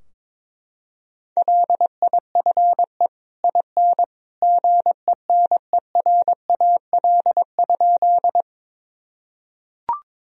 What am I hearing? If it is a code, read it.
Morse code, heard as a keyed tone, about 22 wpm, 700 Hz. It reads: LIFE IN GENERAL?
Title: Sentences from Top 1000 Words - Review Part 3 - 22wpm